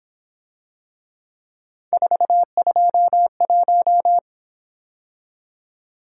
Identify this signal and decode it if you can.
Morse code, heard as a keyed tone, about 27 wpm, 700 Hz. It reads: 421
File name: Speed-Racing - Three Letter-Number Combinations - 22wpm